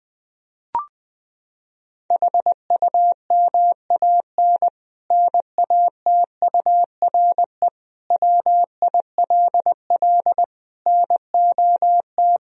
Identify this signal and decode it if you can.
Morse code, heard as a keyed tone, about 20 wpm, 700 Hz. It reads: HUMAN NATURE WILL NOT
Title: Sentences from Top 1000 Words - Review Part 3 - 20wpm